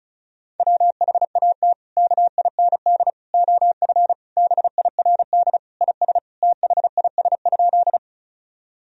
Morse code, heard as a keyed tone, about 36 wpm, 700 Hz. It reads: WHAT KIND OF BIRD IS THIS?
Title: Sentences from Top 1000 Words - Review Part 2 - 35wpm